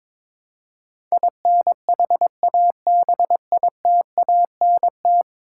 Morse code, heard as a keyed tone, about 22 wpm, 700 Hz. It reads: INHABITANT
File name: practice / 22wpm / 10 Letter Words - First 3 Letters Given - 22wpm